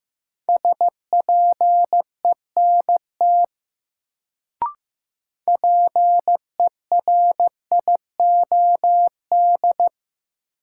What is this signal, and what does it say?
Morse code, heard as a keyed tone, about 15 wpm, 700 Hz. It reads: SPENT PERIOD
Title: Sets of 5 Words - New Word Drill - 15wpm